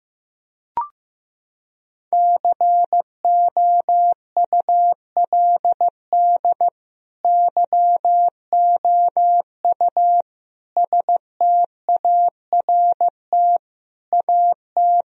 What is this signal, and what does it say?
Morse code, heard as a keyed tone, about 16 wpm, 700 Hz. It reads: COULD YOU START AT
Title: Sentences from Top 1000 Words - Review Part 1 - 15wpm